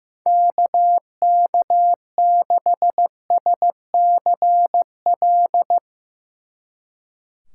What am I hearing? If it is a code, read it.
Morse code, heard as a keyed tone, about 16 wpm, 700 Hz. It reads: KK6SCL
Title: Speed Racing - USA 2x3 Call Signs 15wpm